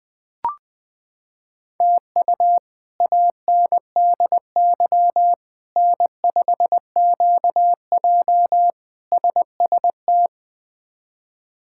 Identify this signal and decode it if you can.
Morse code, heard as a keyed tone, about 20 wpm, 700 Hz. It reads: TU ANDY N5QJ SST
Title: K1USN SST QSO Practice - 20wpm